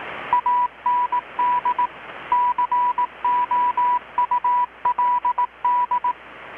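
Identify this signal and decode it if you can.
Morse code, heard as a keyed tone, about 18 wpm, 980 Hz. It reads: AND COULD
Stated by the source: Morse from KPH's Night of Nights XX maritime radiotelegraphy event, received in Chiapas, Mexico about 3,800 km away on a Kenwood R-600 with a grounded 10 m long-wire antenna.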